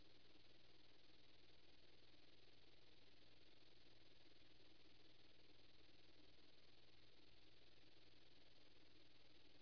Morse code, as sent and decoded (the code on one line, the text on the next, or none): none